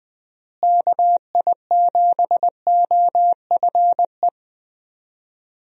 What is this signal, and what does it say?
Morse code, heard as a keyed tone, about 21 wpm, 700 Hz. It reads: KI7OFE
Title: USA Call Signs - 6 Characters - Repeated 3x - 20wpm